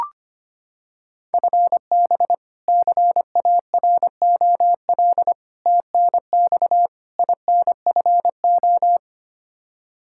Morse code, heard as a keyed tone, about 25 wpm, 700 Hz. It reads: FB CAROL TNX INFO
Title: K1USN SST QSO Practice 25wpm